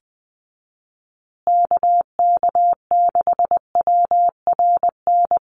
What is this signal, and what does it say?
Morse code, heard as a keyed tone, about 20 wpm, 700 Hz. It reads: KK6WRN
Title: USA Call Signs Repeated 2x 20wpm